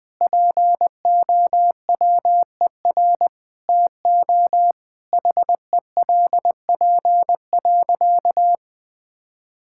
Morse code, heard as a keyed tone, about 21 wpm, 700 Hz. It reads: POWER TO HELP.
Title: Sentences with 8-Word Phrases - 20wpm